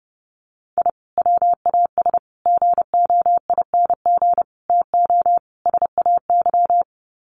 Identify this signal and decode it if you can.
Morse code, heard as a keyed tone, about 30 wpm, 700 Hz. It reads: I WAS GOING TO SAY
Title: Sets of 5 Words 30wpm